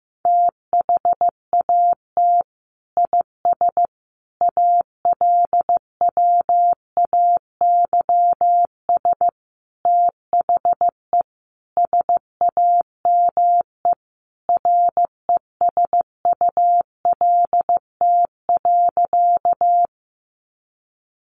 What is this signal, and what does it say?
Morse code, heard as a keyed tone, about 15 wpm, 700 Hz. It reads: THAT IS ALWAYS THE SAME RESULT.
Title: Sentences from Top 700 Words 15wpm